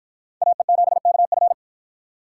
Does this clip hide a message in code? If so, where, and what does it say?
Yes: Morse code from 0.4 to 1.5 s, heard as a keyed tone, about 53 wpm, 700 Hz. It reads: AE7KF